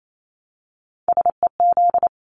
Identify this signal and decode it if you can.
Morse code, heard as a keyed tone, about 28 wpm, 700 Hz. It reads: SEZ
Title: Abbreviations 28wpm